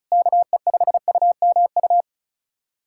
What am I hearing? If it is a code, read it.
Morse code, heard as a keyed tone, about 36 wpm, 700 Hz. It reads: KE5UMU
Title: USA Callsign DE Callsign 35wpm